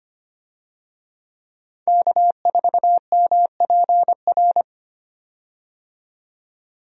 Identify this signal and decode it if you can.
Morse code, heard as a keyed tone, about 25 wpm, 700 Hz. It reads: K4MPR
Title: USA Call Signs - 5 Characters - 25wpm